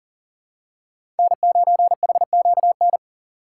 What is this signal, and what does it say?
Morse code, heard as a keyed tone, about 40 wpm, 700 Hz. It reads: N9HQN
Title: USA Call Signs - 6x Character Spacing - 40wpm